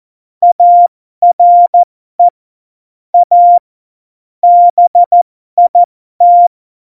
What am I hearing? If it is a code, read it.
Morse code, heard as a keyed tone, about 14 wpm, spaced as for 12 wpm, 700 Hz. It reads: ARE A BIT
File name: Morsekurs / Witze